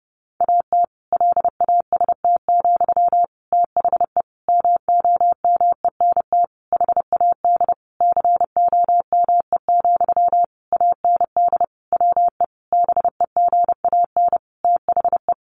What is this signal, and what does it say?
Morse code, heard as a keyed tone, about 30 wpm, 700 Hz. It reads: AT LAST, THE MOMENT HAD COME, AND WE BEGAN THE
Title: Sentences from Top 1500 Words 30wpm